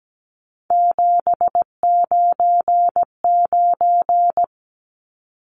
Morse code, heard as a keyed tone, about 17 wpm, 700 Hz. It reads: 799
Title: Numbers - 3-Digit 17wpm